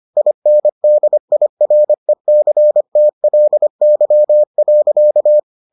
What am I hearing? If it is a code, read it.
Morse code, heard as a keyed tone, about 25 wpm, 600 Hz. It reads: INDIRECTLY.